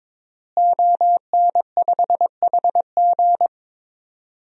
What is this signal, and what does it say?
Morse code, heard as a keyed tone, about 22 wpm, 700 Hz. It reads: ON5HG
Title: DXCC-100 Call Signs - 5 Characters - Repeated 3x - 22wpm